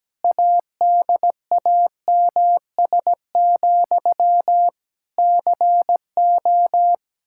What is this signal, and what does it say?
Morse code, heard as a keyed tone, about 17 wpm, 700 Hz. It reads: ADAMS, CO
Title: Practice QSO - Exchange 1 - Beginning - 17wpm